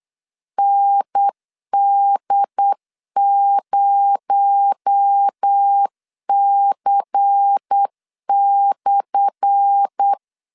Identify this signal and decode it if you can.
Morse code, heard as a keyed tone, about 8 wpm, 800 Hz. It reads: ND0C/
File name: LEZIONI / 10 SIGNS / 10A SLASH 8wpm 3452